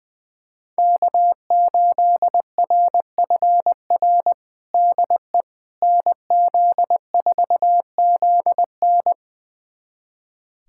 Morse code, heard as a keyed tone, about 20 wpm, 700 Hz. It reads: K8RFR DE NZ4ZN
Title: USA Callsign DE Callsign - 5 Characters - 20wpm